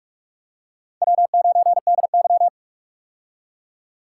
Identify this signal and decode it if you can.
Morse code, heard as a keyed tone, about 43 wpm, 700 Hz. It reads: W9DY
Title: K1USN SST QSO Practice - 45wpm